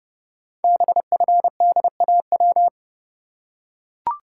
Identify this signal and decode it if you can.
Morse code, heard as a keyed tone, about 30 wpm, 700 Hz. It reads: BFDAW
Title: Code Groups - 5 Letters - 30wpm